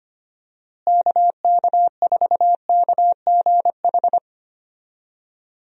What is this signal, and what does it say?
Morse code, heard as a keyed tone, about 25 wpm, 700 Hz. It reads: KK4KGH
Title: USA Call Signs - 6 Characters - Repeated 2x - 25wpm